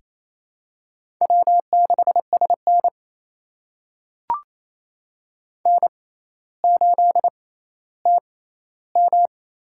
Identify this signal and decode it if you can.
Morse code, heard as a keyed tone, about 28 wpm, spaced as for 14 wpm, 700 Hz. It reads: W6SN N8TM
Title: USA Call Signs - 6x Character Spacing - 28wpm